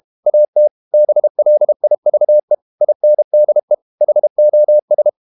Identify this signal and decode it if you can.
Morse code, heard as a keyed tone, about 32 wpm, 600 Hz. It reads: AT BLIVE INDE HOS
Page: Ferie 32 wpm